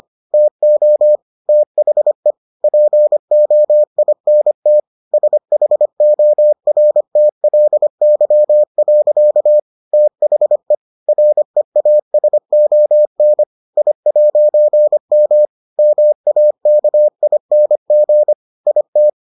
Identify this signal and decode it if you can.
Morse code, heard as a keyed tone, about 25 wpm, 600 Hz. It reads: TO THE POINT SHORTLY. THE REASON I'M MAKING IT